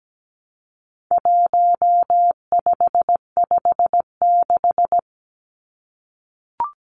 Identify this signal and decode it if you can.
Morse code, heard as a keyed tone, about 17 wpm, 700 Hz. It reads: 1556
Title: Numbers - 4-Digit 17wpm